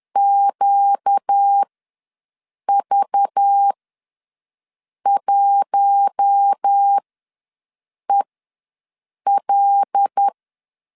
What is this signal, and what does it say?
Morse code, heard as a keyed tone, about 10 wpm, spaced as for 6 wpm, 800 Hz. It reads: QV1EL
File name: LEZIONI / 6 PKX0 / 6 PK ws 10wpm 3745